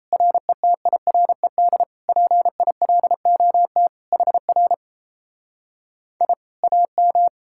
Morse code, heard as a keyed tone, about 38 wpm, 700 Hz. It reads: RETIRED PILOT HR I AM